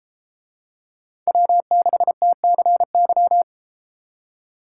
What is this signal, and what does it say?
Morse code, heard as a keyed tone, about 33 wpm, 700 Hz. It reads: W6TCY